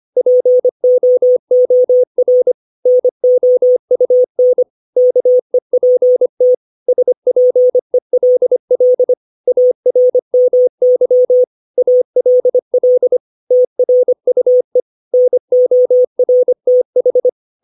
Morse code, heard as a keyed tone, about 25 wpm, 500 Hz. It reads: POOR NOUN KEPT SPELL ARMY ALL TRUE NORTH
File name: 632 words / 632 Words 25WPM